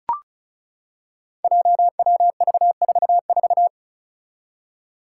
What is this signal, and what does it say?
Morse code, heard as a keyed tone, about 34 wpm, 700 Hz. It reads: JWV44
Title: Five Challenging Letter-Number Combinations 35wpm